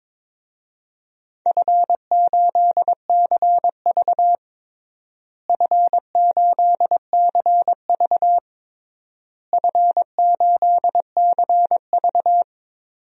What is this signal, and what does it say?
Morse code, heard as a keyed tone, about 22 wpm, 700 Hz. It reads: F8CV F8CV F8CV
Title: DXCC-100 Call Signs - 4 Characters - Repeated 3x - 22wpm